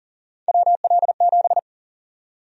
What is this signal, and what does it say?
Morse code, heard as a keyed tone, about 40 wpm, 700 Hz. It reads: WL7
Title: USA Call Sign Prefixes - Repeated 2x - 40wpm